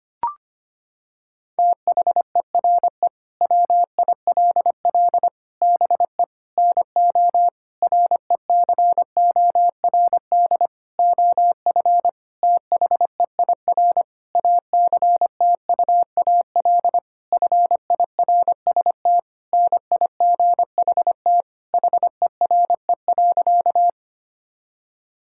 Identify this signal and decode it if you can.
Morse code, heard as a keyed tone, about 25 wpm, 700 Hz. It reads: THERE WILL BE NO RECORD OF THEIR ACTUAL FIRST NIGHT HERE.
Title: Sentences from Top 1000 Words - Review Part 5 - 25wpm